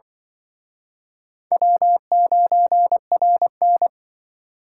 Morse code, heard as a keyed tone, about 24 wpm, 700 Hz. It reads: W9RN
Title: Speed Racing - USA Call Signs 20wpm